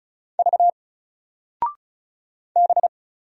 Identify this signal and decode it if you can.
Morse code, heard as a keyed tone, about 36 wpm, 700 Hz. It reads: V B
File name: Single-Letter-Number-35wpm